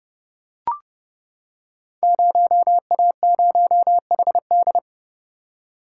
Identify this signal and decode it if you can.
Morse code, heard as a keyed tone, about 29 wpm, 700 Hz. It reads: 0A0HD
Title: Code Groups - 5 Letter:Numbers - 30wpm